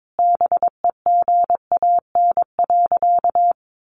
Morse code, heard as a keyed tone, about 22 wpm, 700 Hz. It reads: BEGAN.
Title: Sentences from Top 1100 Words 22wpm